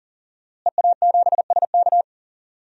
Morse code, heard as a keyed tone, about 38 wpm, 700 Hz. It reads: EA7SK
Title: DXCC-100 Call Signs - 5 Characters - Repeated 3x - 40wpm